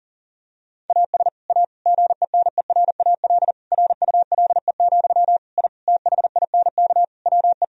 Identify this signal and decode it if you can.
Morse code, heard as a keyed tone, about 40 wpm, 700 Hz. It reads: AS A GENERAL RULE, I THINK WE
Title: Sentences from Top 1000 Words - Review Part 3 - 40wpm